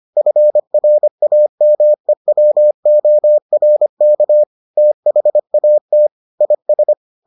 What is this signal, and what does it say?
Morse code, heard as a keyed tone, about 25 wpm, 600 Hz. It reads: FRAMEWORK THAT IS